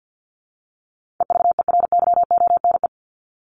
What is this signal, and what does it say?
Morse code, heard as a keyed tone, about 50 wpm, 700 Hz. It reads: EVERYONE
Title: Top 501-750 Words 50wpm